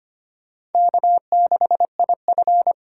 Morse code, heard as a keyed tone, about 32 wpm, 700 Hz. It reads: K6IF